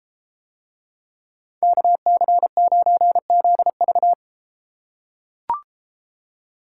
Morse code, heard as a keyed tone, about 32 wpm, 700 Hz. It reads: KC9ZV